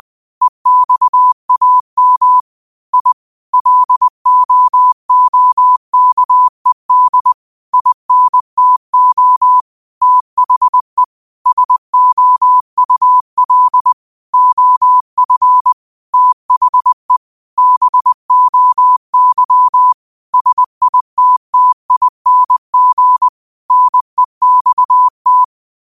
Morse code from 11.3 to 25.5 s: SOUL OF THE BOY SITTING NEXT

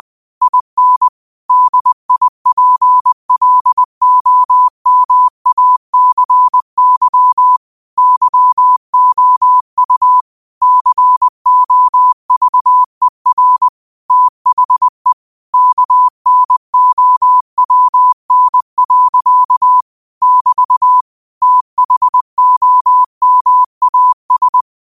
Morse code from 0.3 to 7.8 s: IN DIPLOMACY

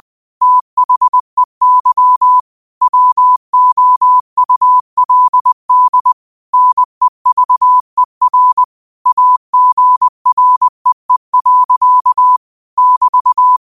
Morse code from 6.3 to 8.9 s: NEVER

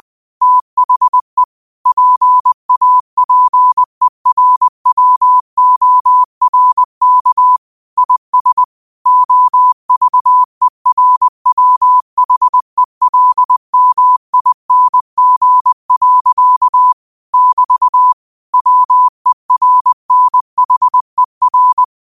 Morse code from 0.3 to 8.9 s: THE PAPERWORK IS

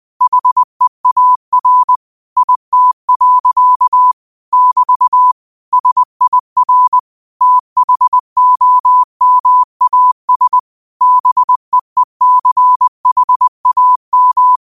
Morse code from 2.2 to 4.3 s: IT.

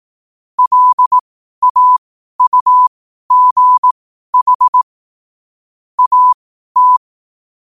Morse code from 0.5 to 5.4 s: LAUGH

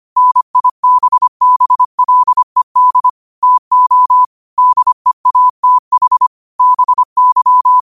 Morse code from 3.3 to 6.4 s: TO DEATH

